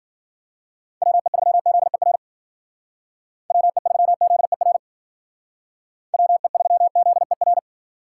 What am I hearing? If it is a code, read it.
Morse code, heard as a keyed tone, about 52 wpm, 700 Hz. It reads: WE3ZER WE3ZER WE3ZER